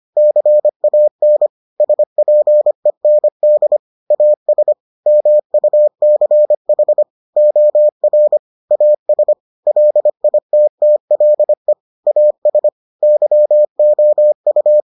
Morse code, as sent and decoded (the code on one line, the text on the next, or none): -.-. .- -. / ... .--. -. -.. / .- ... / -- ..- -.-. .... / --- .-. / .- ... / .-.. .. - - .-.. / .- ... / -.-- --- ..-
CAN SPEND AS MUCH OR AS LITTLE AS YOU